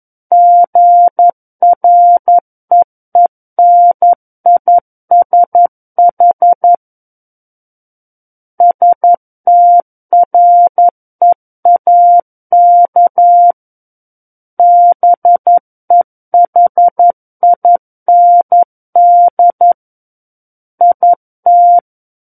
Morse code, heard as a keyed tone, about 11 wpm, 700 Hz. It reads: GREENISH STREAK BEHIND IT
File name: Book One / 02-Chapter 2 - 11 WPM